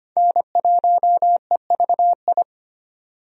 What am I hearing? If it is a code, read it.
Morse code, heard as a keyed tone, about 25 wpm, 690 Hz. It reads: N1EVI
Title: USA Call Signs - 5 Characters - 25wpm